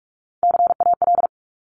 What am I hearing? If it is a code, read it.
Morse code, heard as a keyed tone, about 46 wpm, 690 Hz. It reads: CAL
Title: Top Three Letter Combinations in Words 45wpm